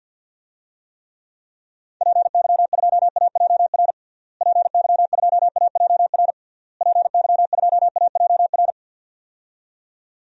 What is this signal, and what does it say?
Morse code, heard as a keyed tone, about 50 wpm, 700 Hz. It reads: PY2AJR PY2AJR PY2AJR
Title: DXCC-100 Call Signs - 6 Characters - Repeated 3x - 50wpm